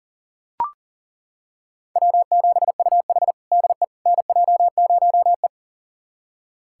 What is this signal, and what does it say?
Morse code, heard as a keyed tone, about 40 wpm, 700 Hz. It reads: W7UH DE NJ0E
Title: USA Callsign DE Callsign - 4 Characters - 40wpm